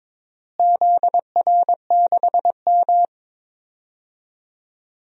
Morse code, heard as a keyed tone, about 22 wpm, 700 Hz. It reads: ZR6M
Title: DXCC-100 Call Signs - 4 Characters - Repeated 2x - 22wpm